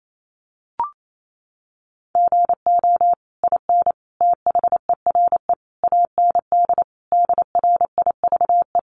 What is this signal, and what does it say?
Morse code, heard as a keyed tone, about 28 wpm, 700 Hz. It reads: GO IN THERE AND DRIVE
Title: Sentences from Top 500 Words 28wpm